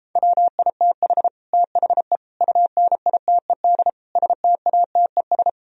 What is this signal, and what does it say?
Morse code, heard as a keyed tone, about 34 wpm, 700 Hz. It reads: WITH THE UNITED STATES